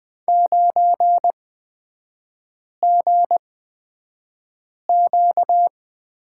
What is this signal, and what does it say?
Morse code, heard as a keyed tone, about 20 wpm, spaced as for 5 wpm, 700 Hz. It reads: 9GQ
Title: 250 Lesson 40e - BK - Three Characters 20wpm